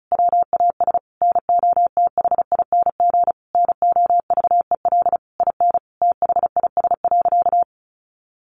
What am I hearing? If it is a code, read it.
Morse code, heard as a keyed tone, about 34 wpm, 700 Hz. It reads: WAS NOTHING NOVEL IN THIS.
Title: Sentences from Top 1300 Words 35wpm